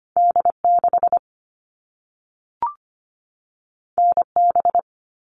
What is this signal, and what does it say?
Morse code, heard as a keyed tone, about 25 wpm, 700 Hz. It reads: D 6 N B
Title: All Cut Numbers - 2-Digit 25wpm